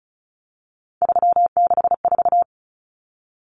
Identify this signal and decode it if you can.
Morse code, heard as a keyed tone, about 34 wpm, 700 Hz. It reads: 364
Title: Common Cut Numbers - 3-Digit 35wpm